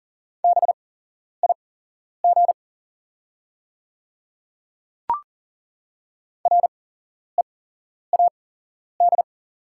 Morse code, heard as a keyed tone, about 40 wpm, spaced as for 11 wpm, 690 Hz. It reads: BIG READ
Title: 8x Character Spacing - Top 300 Words - 40wpm